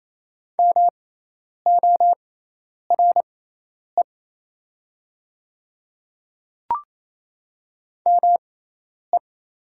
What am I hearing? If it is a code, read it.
Morse code, heard as a keyed tone, about 28 wpm, spaced as for 10 wpm, 700 Hz. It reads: MORE ME
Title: 6x Character Spacing - Top 200 Words - 28wpm